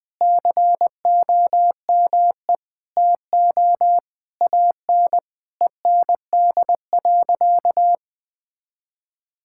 Morse code, heard as a keyed tone, about 20 wpm, 700 Hz. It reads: COME TO AN END.
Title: Sentences from Top 1000 Words - Review Part 6 - 20wpm